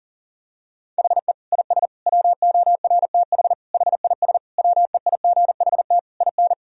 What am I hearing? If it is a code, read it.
Morse code, heard as a keyed tone, about 40 wpm, 700 Hz. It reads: HE IS WORTH HIS WEIGHT IN
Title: Sentences from Top 1000 Words - Review Part 3 - 40wpm